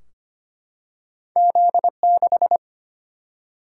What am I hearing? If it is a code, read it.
Morse code, heard as a keyed tone, about 25 wpm, 700 Hz. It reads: Z 6
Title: Single-Letter Single-Number 25wpm